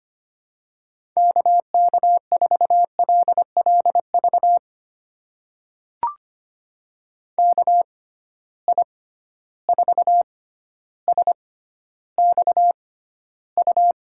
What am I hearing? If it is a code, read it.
Morse code, heard as a keyed tone, about 25 wpm, spaced as for 13 wpm, 700 Hz. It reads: KK4LLV KI4SXU
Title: USA Call Signs - 6 Characters - 6x Character Spacing - 25wpm